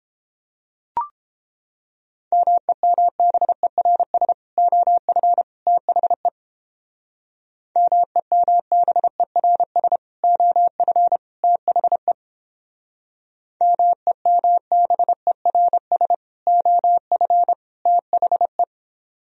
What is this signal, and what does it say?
Morse code, heard as a keyed tone, about 31 wpm, 700 Hz. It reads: MEMBERS OF THE MEMBERS OF THE MEMBERS OF THE